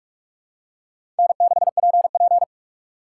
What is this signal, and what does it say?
Morse code, heard as a keyed tone, about 46 wpm, 700 Hz. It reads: N6PP
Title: USA Call Signs - 4 Characters - 4x Character Spacing - 45wpm